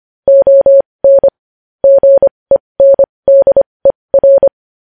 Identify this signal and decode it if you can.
Morse code, heard as a keyed tone, about 25 wpm, spaced as for 20 wpm, 550 Hz. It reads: ON GENDER
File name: podcast / News headlines-1x-25x20